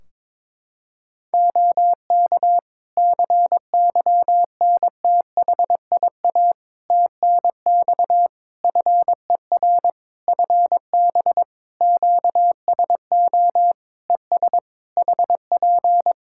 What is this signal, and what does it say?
Morse code, heard as a keyed tone, about 22 wpm, 700 Hz. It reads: OK CYNTHIA TNX FER FB QSO ES HP
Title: K1USN SST QSO Practice 22wpm